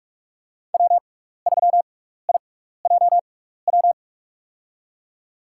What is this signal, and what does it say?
Morse code, heard as a keyed tone, about 41 wpm, spaced as for 15 wpm, 700 Hz. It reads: W3IJW
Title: USA Call Signs - 5 Characters - 6x Character Spacing - 45wpm